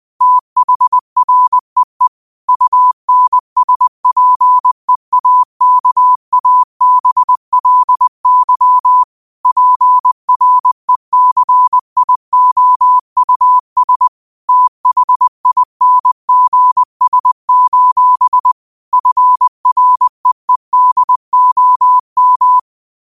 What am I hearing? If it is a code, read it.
Morse code, heard as a keyed tone, about 20 wpm, 1000 Hz. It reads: THREE UNSPEAKABLY PRECIOUS THINGS: FREEDOM